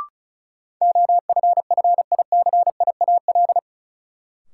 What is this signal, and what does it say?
Morse code, heard as a keyed tone, about 34 wpm, 700 Hz. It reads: OFFICIAL
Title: Sentences from Top 1000 Words - New Word Drill - 35wpm